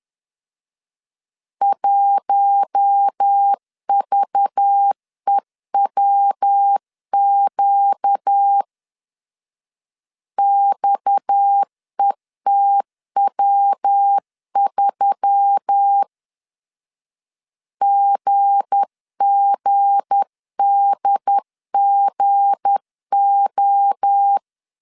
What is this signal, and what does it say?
Morse code, heard as a keyed tone, about 10 wpm, 800 Hz. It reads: 1VEWQ XETW3 GGDGO
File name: LEZIONI / 7 GF9Z / 7 G 10wpm 2501